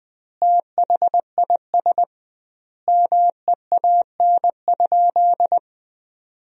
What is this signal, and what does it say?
Morse code, heard as a keyed tone, about 20 wpm, 700 Hz. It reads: THIS MEAN?
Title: 2x Word-Spacing - Sentences from Top 200 Words - 20wpm